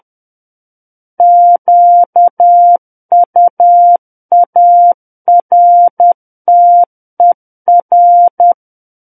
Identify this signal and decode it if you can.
Morse code, heard as a keyed tone, about 10 wpm, 700 Hz. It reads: QUARTER
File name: Book One / 01-Chapter 1 - 10 WPM